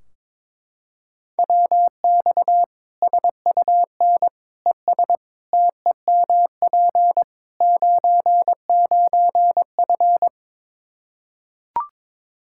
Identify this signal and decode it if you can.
Morse code, heard as a keyed tone, about 22 wpm, 700 Hz. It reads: WX SUN ES TEMP 99F
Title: Practice QSO - Exchange 2 - Info - 22wpm